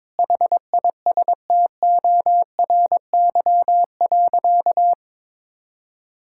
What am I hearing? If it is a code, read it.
Morse code, heard as a keyed tone, about 22 wpm, 700 Hz. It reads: HISTORY.
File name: Sentences with 7-Word Phrases - 22wpm